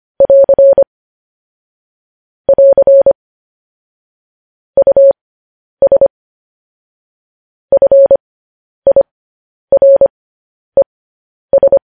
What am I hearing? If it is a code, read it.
Morse code, heard as a keyed tone, about 25 wpm, spaced as for 10 wpm, 550 Hz. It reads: <AR> <AR> US FIRES
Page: news headlines in morse code at 10 wpm